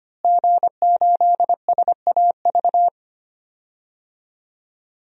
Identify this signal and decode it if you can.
Morse code, heard as a keyed tone, about 25 wpm, 700 Hz. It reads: G8SAV
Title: DXCC-100 Call Signs - 5 Characters - Repeated 2x - 25wpm